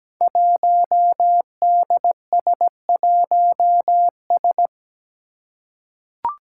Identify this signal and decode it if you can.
Morse code, heard as a keyed tone, about 17 wpm, 700 Hz. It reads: 1DS1S
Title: Five Challenging Letter-Number Combinations 17wpm